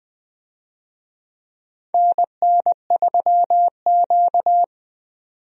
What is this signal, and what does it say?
Morse code, heard as a keyed tone, about 20 wpm, 700 Hz. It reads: NN3Q
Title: USA Call Signs - 4 Characters - 20wpm